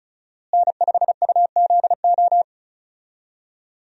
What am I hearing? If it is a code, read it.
Morse code, heard as a keyed tone, about 34 wpm, 700 Hz. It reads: N5UZO
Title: USA Call Signs - 5 Characters - Repeated 2x - 35wpm